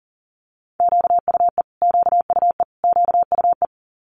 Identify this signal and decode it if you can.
Morse code, heard as a keyed tone, about 40 wpm, 700 Hz. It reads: QUE QUE QUE
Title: Multi-letter Phonemes Repeated 3x 40wpm